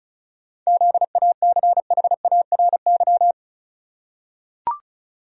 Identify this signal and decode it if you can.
Morse code, heard as a keyed tone, about 34 wpm, 700 Hz. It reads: ZACHARY
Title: USA Common Names - 35wpm